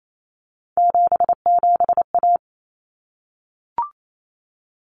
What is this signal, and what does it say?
Morse code, heard as a keyed tone, about 28 wpm, 700 Hz. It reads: 77A
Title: Common Cut Numbers - 3-Digit 28wpm